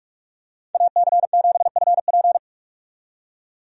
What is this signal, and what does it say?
Morse code, heard as a keyed tone, about 48 wpm, 700 Hz. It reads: AC7FP
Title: USA Call Signs - 5 Characters - 45wpm